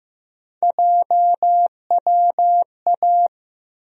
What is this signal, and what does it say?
Morse code, heard as a keyed tone, about 15 wpm, 700 Hz. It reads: JWA
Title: USA Call Sign Postfixes - 3 Letters 15wpm